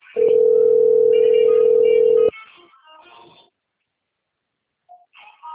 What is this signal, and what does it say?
Ringback tone (North America), 440 + 480 Hz.
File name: rg-1600-unknown- - - .87458